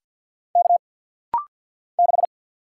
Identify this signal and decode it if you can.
Morse code, heard as a keyed tone, about 50 wpm, 700 Hz. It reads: K 6
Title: Single Letter-Number - Rapid-Fire 50wpm